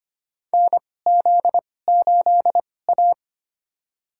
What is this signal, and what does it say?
Morse code, heard as a keyed tone, about 25 wpm, spaced as for 18 wpm, 700 Hz. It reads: NZ8A
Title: USA Call Signs - 4 Characters - 2x Character Spacing - 25wpm